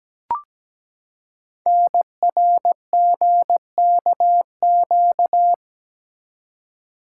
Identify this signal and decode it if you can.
Morse code, heard as a keyed tone, about 17 wpm, 700 Hz. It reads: NRGKQ
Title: Code Groups - 5 Letters - 17wpm